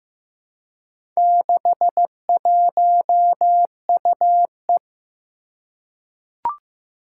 Morse code, heard as a keyed tone, about 15 wpm, 700 Hz. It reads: 61UE
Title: Four Challenging Letter-Number Combinations 15wpm